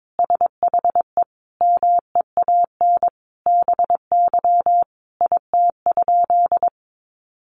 Morse code, heard as a keyed tone, about 22 wpm, 700 Hz. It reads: SHE MEAN BY IT?
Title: Sentences from Top 200 Words 22wpm